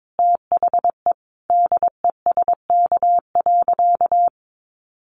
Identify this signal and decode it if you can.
Morse code, heard as a keyed tone, about 22 wpm, 700 Hz. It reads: THE DESK.
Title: A Sound Of Thunder 22wpm